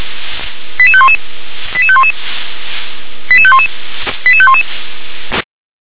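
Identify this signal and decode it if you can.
Five-tone selective call (ZVEI), heard as quick runs of five tones, tones of about 70 ms. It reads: 88511 88511 88511 88511
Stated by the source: Five-tone ZVEI selective call received on an SDR.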